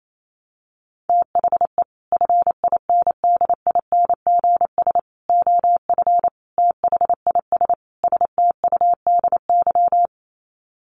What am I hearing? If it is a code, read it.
Morse code, heard as a keyed tone, about 28 wpm, 700 Hz. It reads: THE FINDINGS OF THIS STUDY
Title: Sets of 5 Words Encore2 28wpm